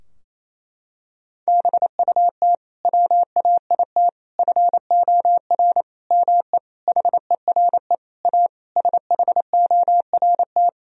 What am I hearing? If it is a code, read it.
Morse code, heard as a keyed tone, about 28 wpm, 700 Hz. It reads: BUT WAIT FOR ME HERE A SHORT
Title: Sentences from Top 1000 Words - Review Part 2 - 28wpm